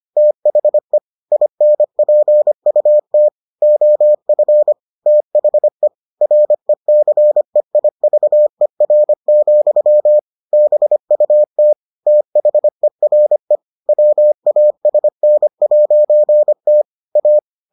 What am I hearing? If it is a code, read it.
Morse code, heard as a keyed tone, about 25 wpm, 600 Hz. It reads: THE INPUT OF THE RECEIVER, BUT THERE WASN'T A